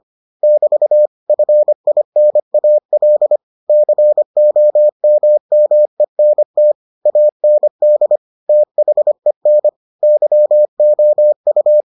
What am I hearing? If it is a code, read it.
Morse code, heard as a keyed tone, about 25 wpm, 600 Hz. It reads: <BT> FINAL COMMENT AND THEN YOU